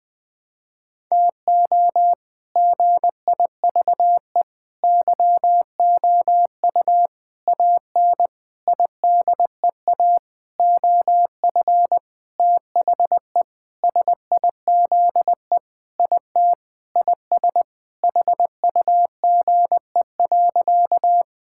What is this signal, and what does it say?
Morse code, heard as a keyed tone, about 20 wpm, 700 Hz. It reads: TO GIVE YOU AN IDEA OF THE SIZE IT IS HUGE.
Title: Sentences with 6-Word Phrases - 20wpm